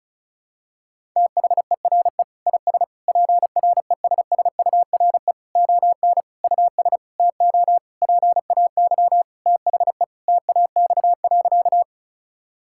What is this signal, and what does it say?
Morse code, heard as a keyed tone, about 34 wpm, 700 Hz. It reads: THERE IS PRESSURE ON US TO PAY THE TAX.
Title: Sentences from Top 1000 Words - Review Part 6 - 35wpm